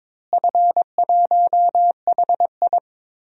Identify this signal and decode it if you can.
Morse code, heard as a keyed tone, about 22 wpm, 700 Hz. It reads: F1HI
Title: DXCC-100 Call Signs - 4 Characters - Repeated 3x - 22wpm